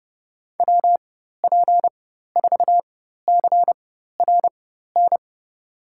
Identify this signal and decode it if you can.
Morse code, heard as a keyed tone, about 30 wpm, spaced as for 14 wpm, 700 Hz. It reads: WP4CRN
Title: USA Call Signs - 6 Characters - 4x Character Spacing - 30wpm